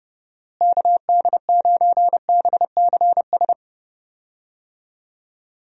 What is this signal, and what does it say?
Morse code, heard as a keyed tone, about 32 wpm, 700 Hz. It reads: KD9BCS
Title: USA Call Signs - 6 Characters - Repeated 3x - 30wpm